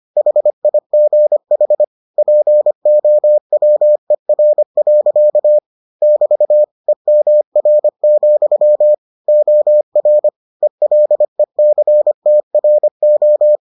Morse code, heard as a keyed tone, about 25 wpm, 600 Hz. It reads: HIGH POWER. <BT> EMR, OR ELECTRO